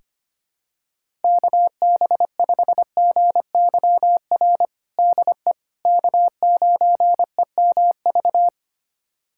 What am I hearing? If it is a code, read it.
Morse code, heard as a keyed tone, about 25 wpm, 700 Hz. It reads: KB5GYR DE K9EMV
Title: USA Callsign DE Callsign 25wpm